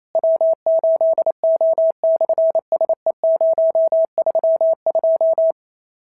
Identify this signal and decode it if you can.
Morse code, heard as a keyed tone, about 28 wpm, 650 Hz. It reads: W8O/SE032
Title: SOTA QSO Practice 28wpm